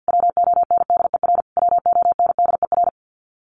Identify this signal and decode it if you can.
Morse code, heard as a keyed tone, about 50 wpm, 700 Hz. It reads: WONDER WONDER
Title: Top 500 Words Repeated 50wpm